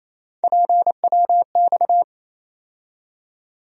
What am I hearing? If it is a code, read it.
Morse code, heard as a keyed tone, about 28 wpm, 700 Hz. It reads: PWX
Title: USA Call Sign Postfixes - 3 Letters 28wpm